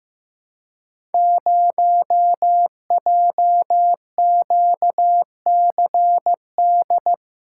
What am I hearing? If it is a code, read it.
Morse code, heard as a keyed tone, about 15 wpm, 700 Hz. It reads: 0JQCD
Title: Code Groups - 5 Letter:Numbers - 15wpm